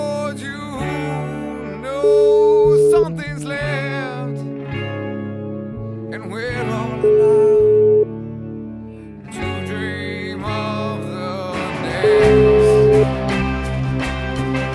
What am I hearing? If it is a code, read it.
Ringback tone (Europe, 425 Hz), with noise.